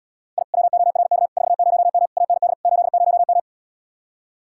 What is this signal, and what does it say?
Morse code, heard as a keyed tone, about 77 wpm, 700 Hz. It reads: I WORK 40M AND 80M